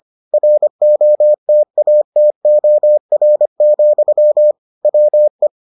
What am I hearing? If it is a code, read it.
Morse code, heard as a keyed tone, about 25 wpm, 600 Hz. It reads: ROTATOR, WE